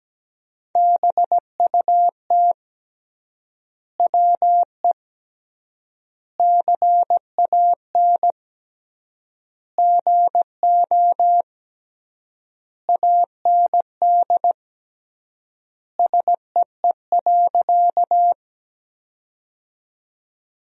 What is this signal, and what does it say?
Morse code, heard as a keyed tone, about 17 wpm, 700 Hz. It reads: BUT WE CAN GO AND SEE.
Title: 3x Word-Spacing - Sentences from Top 100 Words - 17wpm